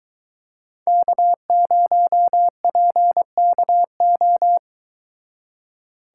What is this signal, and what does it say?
Morse code, heard as a keyed tone, about 22 wpm, 700 Hz. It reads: K0PKO